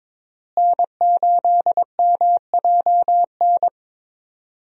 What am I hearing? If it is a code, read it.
Morse code, heard as a keyed tone, about 22 wpm, 700 Hz. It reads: N8MJN
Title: USA Call Signs - 5 Characters - Repeated 3x - 22wpm